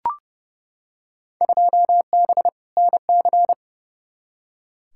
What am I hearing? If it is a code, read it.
Morse code, heard as a keyed tone, about 30 wpm, 700 Hz. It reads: 2B NC
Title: Field Day Exchanges Encore - 30wpm